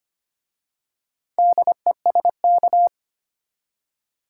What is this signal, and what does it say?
Morse code, heard as a keyed tone, about 25 wpm, 700 Hz. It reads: DESK
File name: practice / 25wpm / Easy Words with Hints - 25wpm